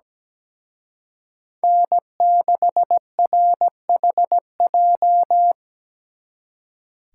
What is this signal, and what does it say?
Morse code, heard as a keyed tone, about 20 wpm, 700 Hz. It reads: N6RHJ
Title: Speed Racing - USA Call Signs 17wpm